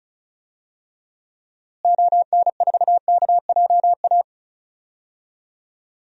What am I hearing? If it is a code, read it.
Morse code, heard as a keyed tone, about 36 wpm, 700 Hz. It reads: ON4KJA
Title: DXCC-100 Call Signs - 6 Characters - Repeated 2x - 35wpm